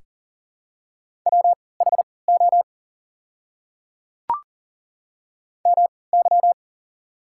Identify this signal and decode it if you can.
Morse code, heard as a keyed tone, about 40 wpm, spaced as for 23 wpm, 700 Hz. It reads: WHO MY
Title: 3x Character Spacing - Top 100 Words - 40wpm